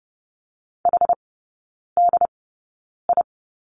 Morse code, heard as a keyed tone, about 30 wpm, spaced as for 9 wpm, 700 Hz. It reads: HDI
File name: Lesson 16e - Letter c - Three Characters 30wpm